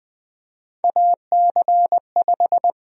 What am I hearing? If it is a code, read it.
Morse code, heard as a keyed tone, about 20 wpm, 700 Hz. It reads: AC5
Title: USA Call Sign Prefixes - 20wpm